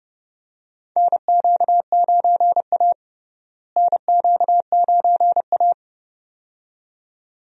Morse code, heard as a keyed tone, about 30 wpm, 700 Hz. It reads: NQ9A NQ9A
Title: USA Call Signs - 4 Characters - Repeated 2x - 30wpm